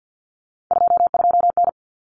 Morse code, heard as a keyed tone, about 50 wpm, 700 Hz. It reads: 22N
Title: Common Cut Numbers - 3-Digit 50wpm